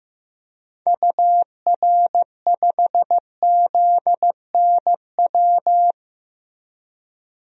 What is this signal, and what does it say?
Morse code, heard as a keyed tone, about 15 wpm, 700 Hz. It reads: UR5ZNW
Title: DXCC-100 Call Signs - 6 Characters - Repeated 3x - 15wpm